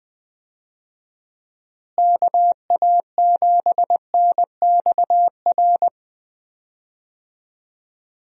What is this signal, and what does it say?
Morse code, heard as a keyed tone, about 20 wpm, 700 Hz. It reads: KA7NXR
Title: USA Call Signs - 6 Characters - Repeated 2x - 20wpm